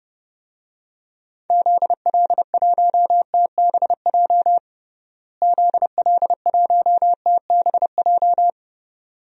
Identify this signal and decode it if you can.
Morse code, heard as a keyed tone, about 30 wpm, 700 Hz. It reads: ZL1TBJ ZL1TBJ
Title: DXCC-100 Call Signs - 6 Characters - Repeated 2x - 30wpm